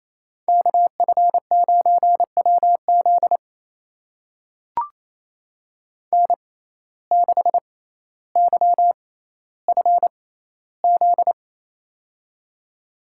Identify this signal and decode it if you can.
Morse code, heard as a keyed tone, about 28 wpm, spaced as for 14 wpm, 700 Hz. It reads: KF9WZ N6YFZ
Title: USA Call Signs - 5 Characters - 6x Character Spacing - 28wpm